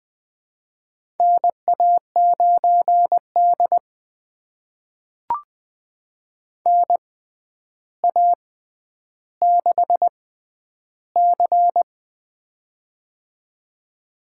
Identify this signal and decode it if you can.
Morse code, heard as a keyed tone, about 21 wpm, spaced as for 10 wpm, 700 Hz. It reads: NA9D NA6C
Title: USA Call Signs - 4 Characters - 6x Character Spacing - 20wpm